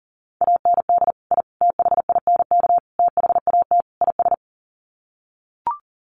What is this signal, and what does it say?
Morse code, heard as a keyed tone, about 40 wpm, 700 Hz. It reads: AND I THINK THAT IS
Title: Sets of 5 Words Encore2 40wpm